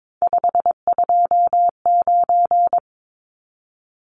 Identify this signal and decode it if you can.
Morse code, heard as a keyed tone, about 22 wpm, 700 Hz. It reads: 529
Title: Numbers - 3-Digit 22wpm